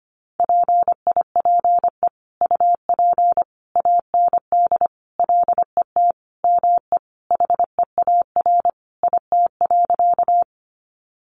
Morse code, heard as a keyed tone, about 25 wpm, 700 Hz. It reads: PIPE UP AND LET ME HEAR IT.
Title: Sentences from Top 1500 Words 25wpm